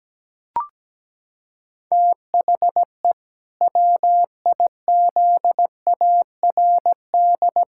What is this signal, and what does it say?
Morse code, heard as a keyed tone, about 17 wpm, 700 Hz. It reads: THE WIZARD
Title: Pangram Challenge v2 - 17wpm